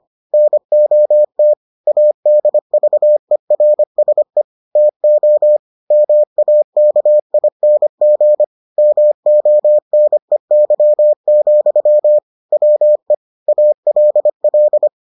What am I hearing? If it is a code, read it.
Morse code, heard as a keyed tone, about 25 wpm, 600 Hz. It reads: NOT ADVERSE TO MAKING MONEY, WE ALL